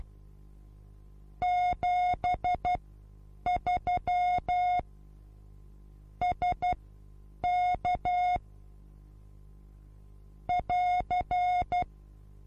Morse code, heard as a keyed tone, about 12 wpm, spaced as for 8 wpm, 720 Hz. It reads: 73 SK <AR>
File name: Onlinelehrgang / morsen / DJ4UF Morsen23